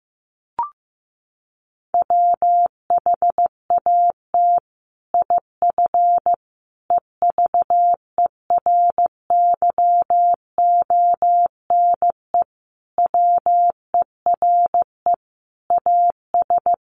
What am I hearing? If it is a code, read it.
Morse code, heard as a keyed tone, about 15 wpm, 700 Hz. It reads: WHAT IF EVERYONE WERE AS
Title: Sentences from Top 1400 Words 15wpm